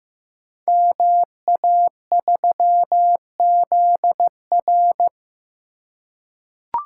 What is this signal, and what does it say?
Morse code, heard as a keyed tone, about 15 wpm, 700 Hz. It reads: MA3ZR
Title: Code Groups - 5 Letter:Numbers - 15wpm